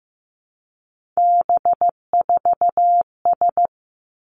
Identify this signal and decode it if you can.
Morse code, heard as a keyed tone, about 15 wpm, 700 Hz. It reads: B4S
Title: Three Challenging Letter-Number Combinations 15wpm